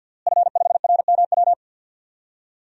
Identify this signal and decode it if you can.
Morse code, heard as a keyed tone, about 53 wpm, 700 Hz. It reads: F5RMW
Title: DXCC-100 Call Signs - 5 Characters - Repeated 3x - 50wpm